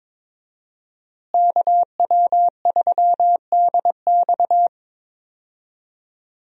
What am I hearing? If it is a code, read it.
Morse code, heard as a keyed tone, about 22 wpm, 700 Hz. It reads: KW3DX
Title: USA Call Signs - 5 Characters - Repeated 3x - 22wpm